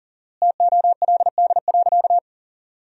Morse code, heard as a keyed tone, about 40 wpm, 700 Hz. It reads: TOLD.
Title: 4x Word-Spacing - Sentences from Top 400 Words - 40wpm